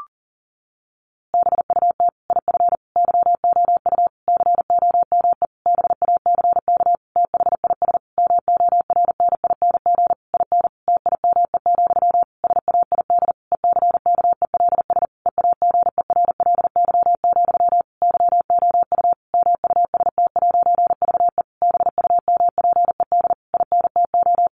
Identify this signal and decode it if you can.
Morse code, heard as a keyed tone, about 40 wpm, 700 Hz. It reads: BUT IF YOU COME BACK THIS MORNING IN TIME, SAID ECKELS EAGERLY, YOU MUST'VE BUMPED INTO
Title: A Sound Of Thunder 40wpm